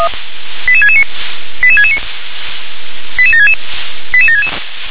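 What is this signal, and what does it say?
Five-tone selective call (ZVEI), heard as quick runs of five tones, tones of about 70 ms. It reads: 88669 88669 88766 88766
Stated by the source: ZVEI five-tone call received on SDR.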